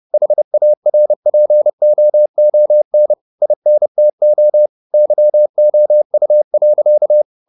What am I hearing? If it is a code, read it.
Morse code, heard as a keyed tone, about 30 wpm, 600 Hz. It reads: HARPOON INTO YOU.
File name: the old man and the sea / the old man and the sea 1 30wpm